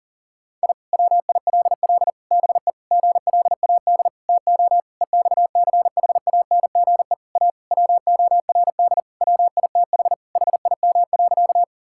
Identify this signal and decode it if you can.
Morse code, heard as a keyed tone, about 40 wpm, 700 Hz. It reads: I WILL BE GLAD TO EXCHANGE A WORD WITH HIM.
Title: Sentences from Top 1000 Words - Review Part 6 - 40wpm